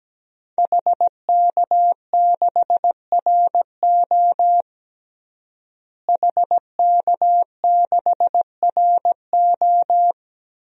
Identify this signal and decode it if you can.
Morse code, heard as a keyed tone, about 17 wpm, 700 Hz. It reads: HK6RO HK6RO
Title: DXCC-100 Call Signs - 5 Characters - Repeated 3x - 17wpm